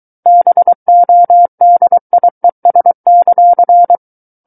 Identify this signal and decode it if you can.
Morse code, heard as a keyed tone, about 23 wpm, 700 Hz. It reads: BODIES;
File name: Book One / 14-Chapter 14 - 23 WPM